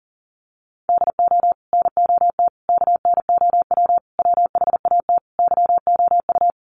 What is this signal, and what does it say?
Morse code, heard as a keyed tone, about 40 wpm, 700 Hz. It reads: DO NOT KNOW WHAT YOU
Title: Sets of 5 Words 40wpm